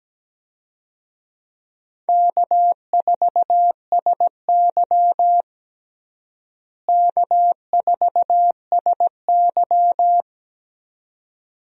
Morse code, heard as a keyed tone, about 17 wpm, 700 Hz. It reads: K4SY K4SY
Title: USA Call Signs - 4 Characters - Repeated 2x - 17wpm